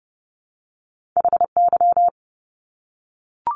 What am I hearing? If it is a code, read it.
Morse code, heard as a keyed tone, about 30 wpm, 700 Hz. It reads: HY
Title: Top Two Letter Combinations in Words 30wpm